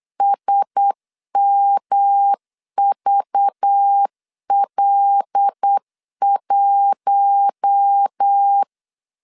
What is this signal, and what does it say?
Morse code, heard as a keyed tone, about 8 wpm, 800 Hz. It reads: SMVL1
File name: LEZIONI / 8 YJ45 / 8 YJ4 8wpm 3246